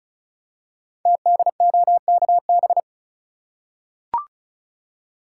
Morse code, heard as a keyed tone, about 36 wpm, 700 Hz. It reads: TDOKB
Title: Code Groups - 5 Letters - 35wpm